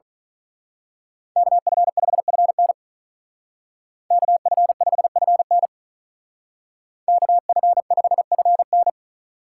Morse code, heard as a keyed tone, about 41 wpm, 700 Hz. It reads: KF5FN KF5FN KF5FN
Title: Speed Racing - USA 2x2 Call Signs 35wpm